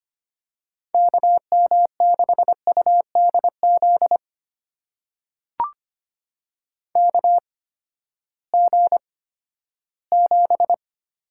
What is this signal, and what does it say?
Morse code, heard as a keyed tone, about 25 wpm, spaced as for 14 wpm, 700 Hz. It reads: KM6UDZ KG7
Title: USA Call Signs - 6 Characters - 8x Character Spacing - 25wpm